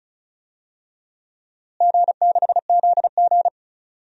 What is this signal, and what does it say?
Morse code, heard as a keyed tone, about 36 wpm, 700 Hz. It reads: G6ZG
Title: DXCC-100 Call Signs - 4 Characters - Repeated 3x - 35wpm